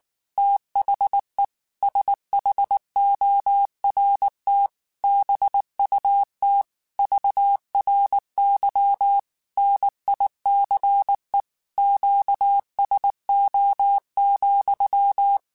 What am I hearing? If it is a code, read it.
Morse code, heard as a keyed tone, about 19 wpm, 800 Hz. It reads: THE SHORT BUT VRY NICE QSO,
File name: QSOs / QSOs-mp3s / WA7QOX 2@18